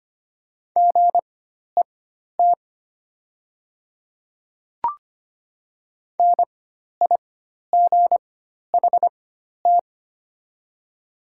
Morse code, heard as a keyed tone, about 25 wpm, spaced as for 12 wpm, 700 Hz. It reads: GET NIGHT
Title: 4x Character Spacing - Top 300 Words - 25wpm